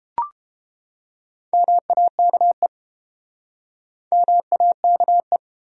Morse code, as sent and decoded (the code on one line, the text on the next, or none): -- .- -.- . / -- .- -.- .
MAKE MAKE